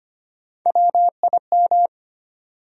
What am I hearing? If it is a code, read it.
Morse code, heard as a keyed tone, about 25 wpm, 700 Hz. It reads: WIM
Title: USA Call Sign Postfixes - 3 Letters 25wpm